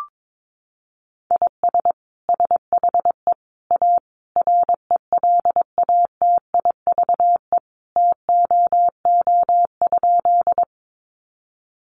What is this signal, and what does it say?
Morse code, heard as a keyed tone, about 22 wpm, 700 Hz. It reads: IS SHE A RELATIVE TOO?
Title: Sentences from Top 1400 Words 22wpm